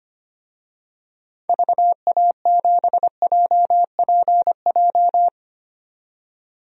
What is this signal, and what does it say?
Morse code, heard as a keyed tone, about 25 wpm, 700 Hz. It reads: VA7JPJ
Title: DXCC-100 Call Signs - 6 Characters - 25wpm